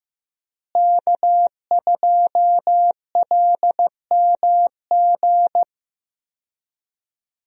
Morse code, heard as a keyed tone, about 15 wpm, 700 Hz. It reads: K2LMG
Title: USA Call Signs - 5 Characters - Repeated 3x - 15wpm